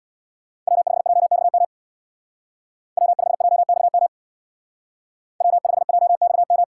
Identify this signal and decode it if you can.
Morse code, heard as a keyed tone, about 62 wpm, 700 Hz. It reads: W5JXM W5JXM W5JXM
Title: Speed Racing - USA 1x3 Call Signs 50wpm